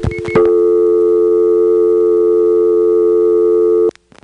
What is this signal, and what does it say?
Dial tone (North America), 350 + 440 Hz.
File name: GroupBell / MariettaMap-unfinished / Marietta1